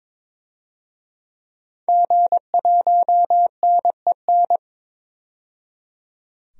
Morse code, heard as a keyed tone, about 22 wpm, 700 Hz. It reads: G1NEN
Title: DXCC-100 Call Signs - 5 Characters - 22wpm